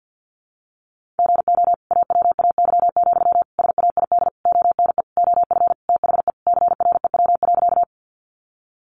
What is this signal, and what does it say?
Morse code, heard as a keyed tone, about 50 wpm, 700 Hz. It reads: GO AWAY, SAID ONE OF THE CREW.
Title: Sentences from Top 1500 Words 50wpm